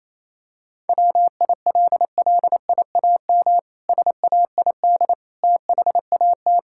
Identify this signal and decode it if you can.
Morse code, heard as a keyed tone, about 28 wpm, 700 Hz. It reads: WILLIAM SAID THAT
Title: Pangram Challenge v2 - 28wpm